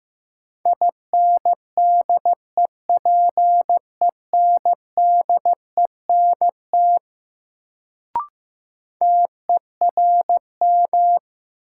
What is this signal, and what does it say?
Morse code, heard as a keyed tone, about 15 wpm, 700 Hz. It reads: INDEPENDENT TERM
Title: Sentences from Top 900 Words - New Word Drill - 15wpm